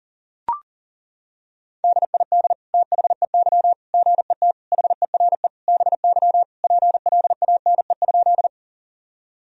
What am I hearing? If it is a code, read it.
Morse code, heard as a keyed tone, about 40 wpm, 700 Hz. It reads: DID THEY GET HERE BY PLANE?
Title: Sentences from Top 1000 Words - Review Part 3 - 40wpm